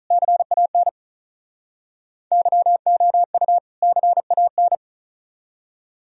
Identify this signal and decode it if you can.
Morse code, heard as a keyed tone, about 45 wpm, 700 Hz. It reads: CAN YOU CAN